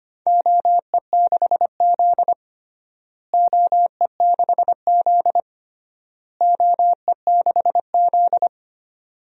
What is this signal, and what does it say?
Morse code, heard as a keyed tone, about 25 wpm, 700 Hz. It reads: OE6Z OE6Z OE6Z
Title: DXCC-100 Call Signs - 4 Characters - Repeated 3x - 25wpm